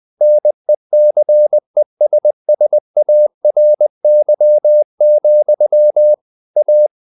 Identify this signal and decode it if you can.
Morse code, heard as a keyed tone, about 20 wpm, 600 Hz. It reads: NECESSARY, A
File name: the old man and the sea / the old man and the sea part 1 20wpm